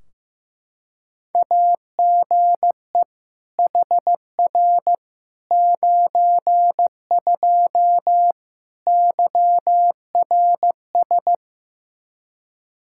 Morse code, heard as a keyed tone, about 15 wpm, 700 Hz. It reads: AGE HR 92 YRS
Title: Practice QSO - Exchange 3 - More Info - 15wpm